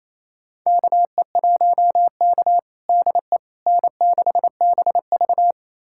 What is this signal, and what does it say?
Morse code, heard as a keyed tone, about 28 wpm, 700 Hz. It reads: KE1K DE N6BV
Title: USA Callsign DE Callsign - 4 Characters - 28wpm